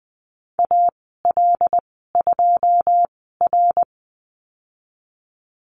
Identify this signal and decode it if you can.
Morse code, heard as a keyed tone, about 20 wpm, spaced as for 15 wpm, 700 Hz. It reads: AL2R
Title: 140 Lesson 23g - Letter f - Call Signs 20wpm